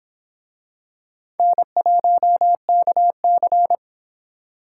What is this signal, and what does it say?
Morse code, heard as a keyed tone, about 23 wpm, 700 Hz. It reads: N1KC